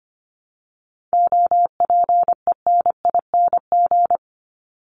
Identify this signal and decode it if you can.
Morse code, heard as a keyed tone, about 25 wpm, 700 Hz. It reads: OPENING
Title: ING Suffix - Top 100 Words with Base Words 25wpm